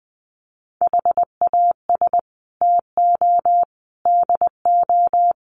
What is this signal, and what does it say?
Morse code, heard as a keyed tone, about 20 wpm, 700 Hz. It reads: HAS TO DO
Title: Sets of 5 Words 20wpm